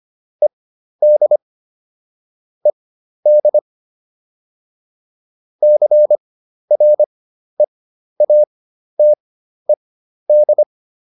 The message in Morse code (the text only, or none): ED ED CREATED